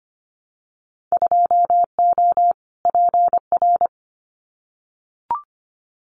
Morse code, heard as a keyed tone, about 24 wpm, 700 Hz. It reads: 2O PR
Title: Winter Field Day Exchanges Encore 25wpm